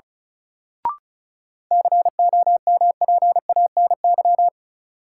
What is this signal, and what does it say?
Morse code, heard as a keyed tone, about 36 wpm, 700 Hz. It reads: COMPANY
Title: Sets of 2 Words - New Word Drill - 35wpm